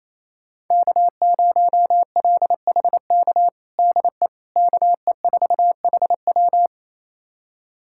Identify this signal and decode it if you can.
Morse code, heard as a keyed tone, about 28 wpm, 700 Hz. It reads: K0LHK DE KE4HW
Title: USA Callsign DE Callsign - 5 Characters - 28wpm